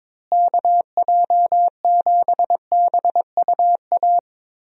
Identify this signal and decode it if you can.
Morse code, heard as a keyed tone, about 21 wpm, 700 Hz. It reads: KJ7BUA
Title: USA Call Signs - 6 Characters - Repeated 3x - 22wpm